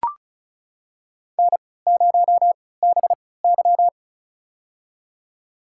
Morse code, heard as a keyed tone, about 34 wpm, spaced as for 20 wpm, 700 Hz. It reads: N0BY
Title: USA Call Signs - 4 Characters - 3x Character Spacing - 35wpm